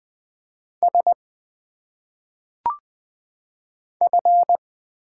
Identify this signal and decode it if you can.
Morse code, heard as a keyed tone, about 20 wpm, 700 Hz. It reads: S F
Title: Single Letters - Phonetic Pronunciation 20wpm